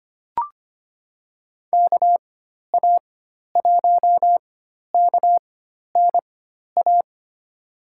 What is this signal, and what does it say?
Morse code, heard as a keyed tone, about 25 wpm, spaced as for 12 wpm, 700 Hz. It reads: KA1KNA
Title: USA Call Signs - 6 Characters - 4x Character Spacing - 25wpm